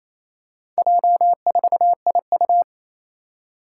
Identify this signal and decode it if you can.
Morse code, heard as a keyed tone, about 29 wpm, 700 Hz. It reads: J4IU
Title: Four Challenging Letter-Number Combinations 28wpm